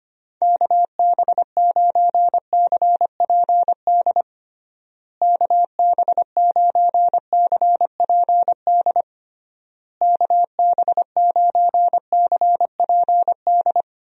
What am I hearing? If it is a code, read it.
Morse code, heard as a keyed tone, about 25 wpm, 700 Hz. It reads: KB9CPD KB9CPD KB9CPD
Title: USA Call Signs - 6 Characters - Repeated 3x - 25wpm